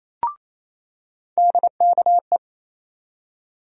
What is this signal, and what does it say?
Morse code, heard as a keyed tone, about 28 wpm, 700 Hz. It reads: DKE
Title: USA Call Sign Postfixes - 3 Letters 28wpm